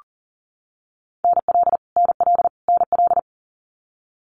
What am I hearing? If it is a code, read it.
Morse code, heard as a keyed tone, about 40 wpm, 700 Hz. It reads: NL NL NL
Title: Canadian Provinces and Territories - Repeated 3x - 40wpm